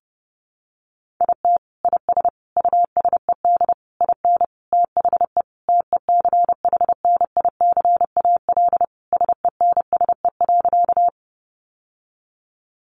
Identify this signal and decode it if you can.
Morse code, heard as a keyed tone, about 28 wpm, 700 Hz. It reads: IT IS USED IN THE TECHNICAL SENSE.
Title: Sentences from Top 1100 Words 30wpm